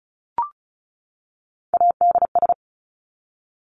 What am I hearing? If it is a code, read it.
Morse code, heard as a keyed tone, about 36 wpm, 700 Hz. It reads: ADS
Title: Abbreviations 35wpm